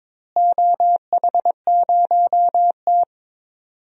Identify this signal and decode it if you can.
Morse code, heard as a keyed tone, about 22 wpm, 700 Hz. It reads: OH0T
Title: DXCC-100 Call Signs - 4 Characters - Repeated 2x - 22wpm